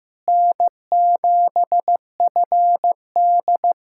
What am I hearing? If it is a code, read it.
Morse code, heard as a keyed tone, about 15 wpm, 700 Hz. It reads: N7FD
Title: USA Callsign DE Callsign - 4 Characters - 15wpm